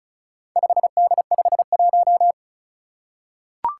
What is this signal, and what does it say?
Morse code, heard as a keyed tone, about 36 wpm, 700 Hz. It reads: 5D51
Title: Four Challenging Letter-Number Combinations 35wpm